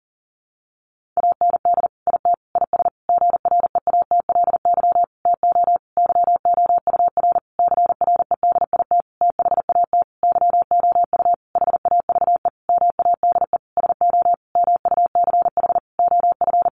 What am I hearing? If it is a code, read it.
Morse code, heard as a keyed tone, about 40 wpm, 700 Hz. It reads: AND IT IS GREATLY TO YOUR CREDIT THAT YOU HAVE MADE SO MUCH OF
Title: Sentences from Top 1400 Words 40wpm